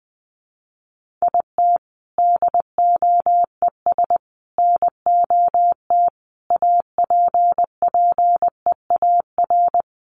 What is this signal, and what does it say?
Morse code, heard as a keyed tone, about 20 wpm, 700 Hz. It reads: IT DOES NOT APPEAR
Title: Sentences from Top 1400 Words 20wpm